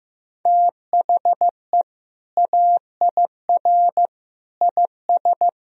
Morse code, heard as a keyed tone, about 15 wpm, 700 Hz. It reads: THE AIR IS
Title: Sentences from Top 1000 Words - Review Part 1 - 15wpm